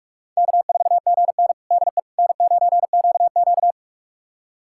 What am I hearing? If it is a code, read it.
Morse code, heard as a keyed tone, about 45 wpm, 700 Hz. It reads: K4GN DE N9QQ
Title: USA Callsign DE Callsign - 4 Characters - 45wpm